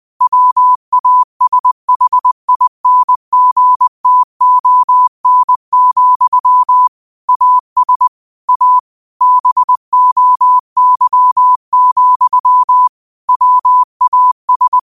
Morse code, heard as a keyed tone, about 20 wpm, 1000 Hz. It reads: WASHINGTON, AS A BOY, WAS